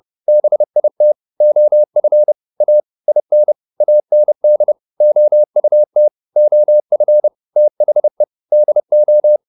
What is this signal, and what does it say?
Morse code, heard as a keyed tone, about 30 wpm, 600 Hz. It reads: BIT OF A IN AND OUT OF THE DO